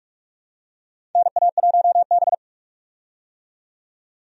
Morse code, heard as a keyed tone, about 46 wpm, 700 Hz. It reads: NA1B
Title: USA Call Signs - 4 Characters - 45wpm